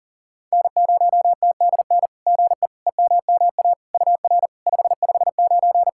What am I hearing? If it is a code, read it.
Morse code, heard as a keyed tone, about 40 wpm, 700 Hz. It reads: N0TDN GE EMMA UR 559
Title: SOTA QSO Practice 40wpm